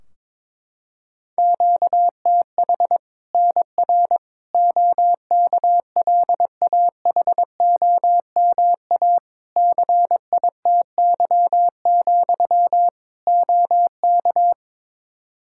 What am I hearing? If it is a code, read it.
Morse code, heard as a keyed tone, about 22 wpm, 700 Hz. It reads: QTH NR OKLAHOMA CITY, OK
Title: Practice QSO - Exchange 1 - Beginning - 22wpm